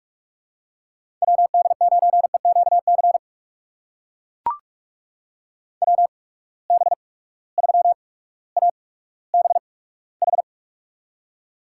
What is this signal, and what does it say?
Morse code, heard as a keyed tone, about 44 wpm, spaced as for 19 wpm, 700 Hz. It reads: WD9EQC WB3ABH
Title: USA Call Signs - 6 Characters - 8x Character Spacing - 45wpm